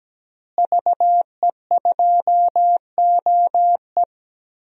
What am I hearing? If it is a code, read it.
Morse code, heard as a keyed tone, about 17 wpm, 700 Hz. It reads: VE2OE
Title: DXCC-100 Call Signs - 5 Characters - Repeated 2x - 17wpm